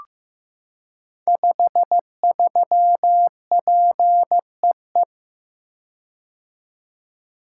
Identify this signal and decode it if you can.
Morse code, heard as a keyed tone, about 15 wpm, 700 Hz. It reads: 53PEE
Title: Code Groups - 5 Letter:Numbers - 15wpm